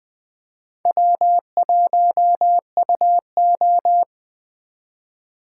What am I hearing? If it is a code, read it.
Morse code, heard as a keyed tone, about 21 wpm, 700 Hz. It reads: W1UO
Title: Speed Racing - USA 1x2 Call Signs 17wpm